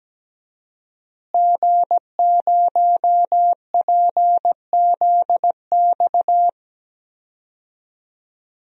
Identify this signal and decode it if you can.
Morse code, heard as a keyed tone, about 17 wpm, 700 Hz. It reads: G0PZX
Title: DXCC-100 Call Signs - 5 Characters - Repeated 3x - 17wpm